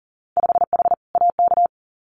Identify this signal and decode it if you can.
Morse code, heard as a keyed tone, about 40 wpm, 700 Hz. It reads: 5H AK
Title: Winter Field Day Exchanges Encore 40wpm